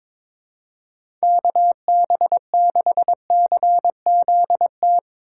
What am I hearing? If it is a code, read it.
Morse code, heard as a keyed tone, about 22 wpm, 700 Hz. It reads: KB6CZT
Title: USA Call Signs - 2x Character Spacing - 22wpm